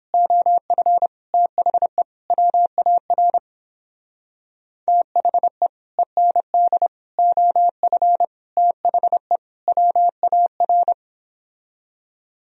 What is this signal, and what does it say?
Morse code, heard as a keyed tone, about 29 wpm, 700 Hz. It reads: OF THE WAR THE END OF THE WAR